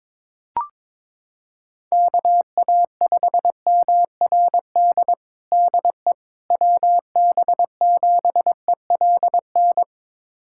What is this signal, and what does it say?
Morse code, heard as a keyed tone, about 22 wpm, 700 Hz. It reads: KA5MRD DE WB7ELN
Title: USA Callsign DE Callsign 22wpm